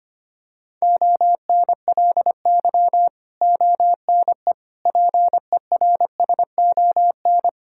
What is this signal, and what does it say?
Morse code, heard as a keyed tone, about 25 wpm, 700 Hz. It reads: ONLY ONE PERSON
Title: Sentences from Top 1000 Words - Review Part 6 - 25wpm